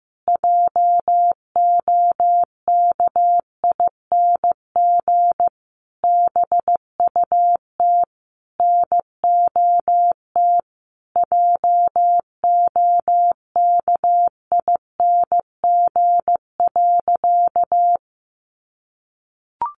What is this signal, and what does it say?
Morse code, heard as a keyed tone, about 15 wpm, 700 Hz. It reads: JOKING BUT NOT JOKING.
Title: A Sound Of Thunder 15wpm